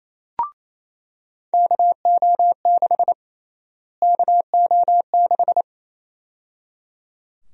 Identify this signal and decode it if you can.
Morse code, heard as a keyed tone, about 28 wpm, 700 Hz. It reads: KO6 KO6
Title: USA Call Sign Prefixes - Repeated 2x - 28wpm